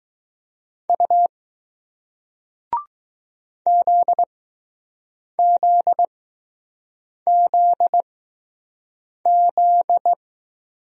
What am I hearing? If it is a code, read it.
Morse code, heard as a keyed tone, about 20 wpm, spaced as for 6 wpm, 700 Hz. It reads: U ZZZZ